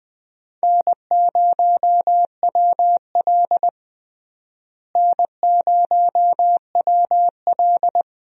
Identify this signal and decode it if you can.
Morse code, heard as a keyed tone, about 20 wpm, 700 Hz. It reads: N0WL N0WL
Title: USA Call Signs - 4 Characters - Repeated 2x - 20wpm